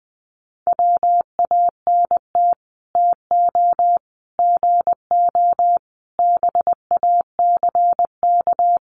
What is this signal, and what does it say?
Morse code, heard as a keyed tone, about 20 wpm, 700 Hz. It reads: WANT TO GO BACK
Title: Sets of 5 Words 20wpm